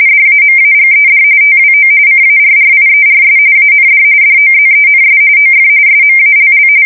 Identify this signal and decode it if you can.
RTTY, about 45.45 baud. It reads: DAMAGED TWO CARRIERS AND DOZENS OF SMALL
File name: fsk170 / battleshipnj